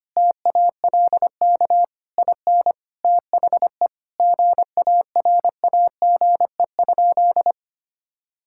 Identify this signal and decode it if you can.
Morse code, heard as a keyed tone, about 25 wpm, 700 Hz. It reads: TALK IN THE GARAGE?
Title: Sentences from Top 1600 Words 25wpm